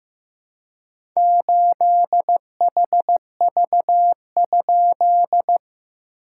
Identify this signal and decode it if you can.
Morse code, heard as a keyed tone, about 15 wpm, 700 Hz. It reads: 8HV?
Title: SOTA QSO Practice 15wpm